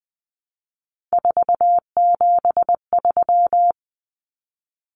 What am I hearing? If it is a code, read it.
Morse code, heard as a keyed tone, about 20 wpm, 700 Hz. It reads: 473
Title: Numbers - 3-Digit 20wpm